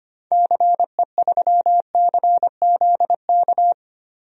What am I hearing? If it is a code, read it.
Morse code, heard as a keyed tone, about 25 wpm, 700 Hz. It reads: CE3CZK
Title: DXCC-100 Call Signs - 6 Characters - 25wpm